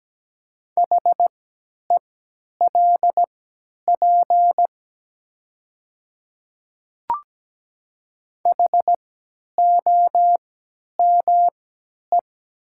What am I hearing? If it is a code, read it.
Morse code, heard as a keyed tone, about 17 wpm, spaced as for 10 wpm, 700 Hz. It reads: HELP HOME
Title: 3x Character Spacing - Top 300 Words - 17wpm